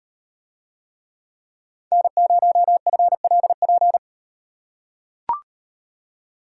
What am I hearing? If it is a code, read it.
Morse code, heard as a keyed tone, about 38 wpm, 700 Hz. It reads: N0FLP